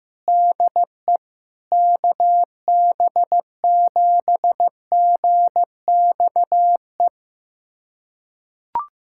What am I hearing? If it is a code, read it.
Morse code, heard as a keyed tone, about 15 wpm, 700 Hz. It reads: DE KB7GXE
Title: USA Callsign DE Callsign - 6 Characters - 15wpm